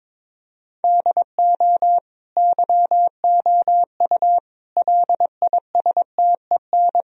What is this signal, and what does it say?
Morse code, heard as a keyed tone, about 22 wpm, 700 Hz. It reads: DO YOU LISTEN
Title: Sentences from Top 1000 Words - Review Part 3 - 22wpm